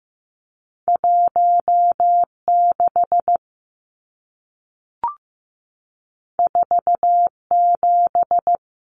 Morse code, heard as a keyed tone, about 15 wpm, 700 Hz. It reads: 1 6 4 7
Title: Numbers - 2-Digit 15wpm